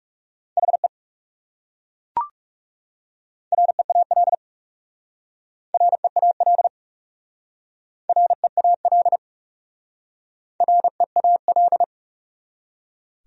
Morse code, heard as a keyed tone, about 37 wpm, 700 Hz. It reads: HE REAL REAL REAL REAL
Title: Speed-Racing Top 300 Words 30wpm